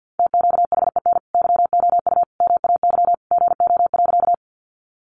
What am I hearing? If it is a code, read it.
Morse code, heard as a keyed tone, about 50 wpm, 700 Hz. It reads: TQHEN YOU MAY GO.
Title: Sentences from Top 100 Words 50wpm